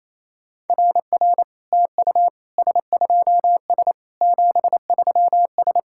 Morse code, heard as a keyed tone, about 27 wpm, 690 Hz. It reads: RR TU S2S 73S
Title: SOTA QSO Practice 28wpm